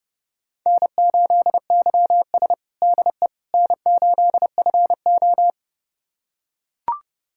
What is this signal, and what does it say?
Morse code, heard as a keyed tone, about 30 wpm, 700 Hz. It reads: N8YS DE N8FO
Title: USA Callsign DE Callsign - 4 Characters - 30wpm